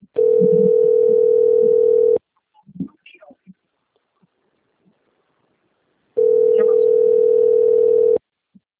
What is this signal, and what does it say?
Ringback tone (North America), 440 + 480 Hz, with speech.
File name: rg-1600-unknown- - - .64879